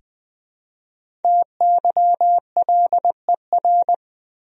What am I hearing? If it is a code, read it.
Morse code, heard as a keyed tone, about 20 wpm, 700 Hz. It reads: TYLER
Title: USA Common Names - Max 5 Letters - 20wpm